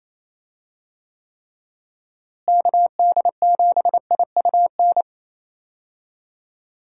Morse code, heard as a keyed tone, about 27 wpm, 700 Hz. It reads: KD7IUN
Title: USA Call Signs - 6 Characters - Repeated 3x - 28wpm